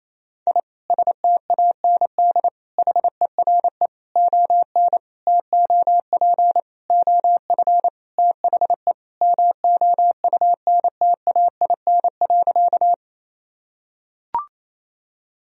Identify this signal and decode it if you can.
Morse code, heard as a keyed tone, about 28 wpm, 700 Hz. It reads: I STAND HERE ON TOP OF THE MOUNTAIN.
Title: Sentences from Top 1000 Words - Review Part 1 - 28wpm